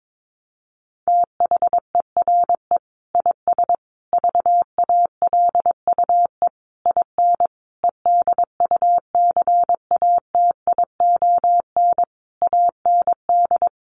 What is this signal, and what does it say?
Morse code, heard as a keyed tone, about 22 wpm, 700 Hz. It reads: THERE IS VALUE IN EDUCATION AND
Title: Sentences from Top 1100 Words 22wpm